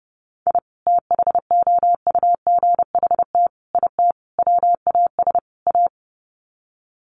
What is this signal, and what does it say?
Morse code, heard as a keyed tone, about 30 wpm, 700 Hz. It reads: I THOUGHT IT WAS A
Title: Sets of 5 Words 30wpm